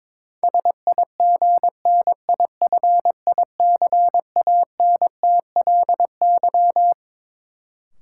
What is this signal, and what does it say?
Morse code, heard as a keyed tone, about 22 wpm, 700 Hz. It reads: SIGNIFICANTLY
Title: Sentences from Top 1500 Words - New Word Drill - 22wpm